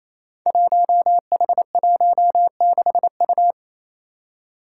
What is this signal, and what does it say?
Morse code, heard as a keyed tone, about 27 wpm, 700 Hz. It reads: 1H16U
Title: Five Challenging Letter-Number Combinations 28wpm